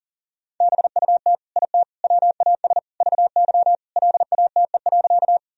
Morse code, heard as a keyed tone, about 40 wpm, 700 Hz. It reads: BUT IT WAS VY LATE.
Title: Sentences from Top 300 Words and Abbreviations 40wpm